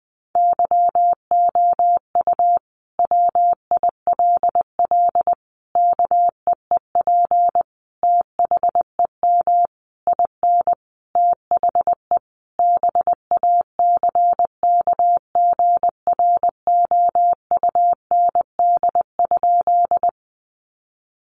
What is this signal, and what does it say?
Morse code, heard as a keyed tone, about 20 wpm, 700 Hz. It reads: YOU WILL KEEP THEM IN THE BACKGROUND?
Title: Sentences from Top 1300 Words 20wpm